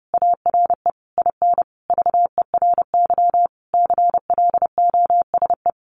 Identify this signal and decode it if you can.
Morse code, heard as a keyed tone, about 30 wpm, 700 Hz. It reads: ARE IN VERY CLOSE
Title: Sentences from Top 1300 Words 30wpm